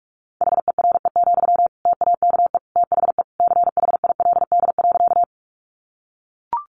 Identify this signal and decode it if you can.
Morse code, heard as a keyed tone, about 44 wpm, 700 Hz. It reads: HERE, TAKE THE CHILD.
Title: Sentences from Top 800 Words 45wpm